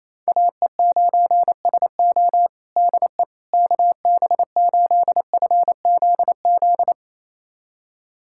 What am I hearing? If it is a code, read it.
Morse code, heard as a keyed tone, about 28 wpm, 700 Hz. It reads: AE9SO DE KB8FZZ
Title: USA Callsign DE Callsign 28wpm